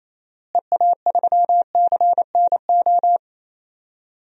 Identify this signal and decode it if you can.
Morse code, heard as a keyed tone, about 28 wpm, 700 Hz. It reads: EA3CNO
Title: DXCC-100 Call Signs - 6 Characters - Repeated 2x - 28wpm